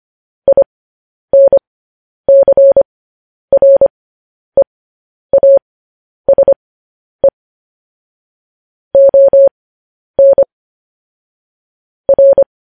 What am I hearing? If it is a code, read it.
Morse code, heard as a keyed tone, about 25 wpm, spaced as for 10 wpm, 550 Hz. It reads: INCREASE ON R
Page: news-headlines-in-morse-code-at-10-wpm